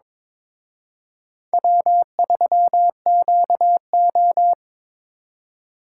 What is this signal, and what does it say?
Morse code, heard as a keyed tone, about 26 wpm, 700 Hz. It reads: W3QO
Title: Speed Racing - USA Call Signs 22wpm